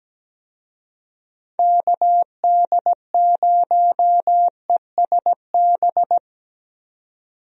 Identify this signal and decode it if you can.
Morse code, heard as a keyed tone, about 17 wpm, 700 Hz. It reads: KD0ESB
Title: USA Call Signs - 6 Characters - 17wpm